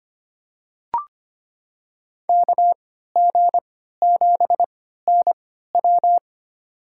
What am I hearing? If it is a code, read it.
Morse code, heard as a keyed tone, about 25 wpm, spaced as for 14 wpm, 700 Hz. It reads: KG7NW
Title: USA Call Signs - 5 Characters - 3x Character Spacing - 25wpm